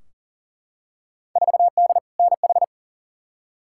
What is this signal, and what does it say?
Morse code, heard as a keyed tone, about 39 wpm, 700 Hz. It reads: 4D NH
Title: Field Day Exchanges Encore - 40wpm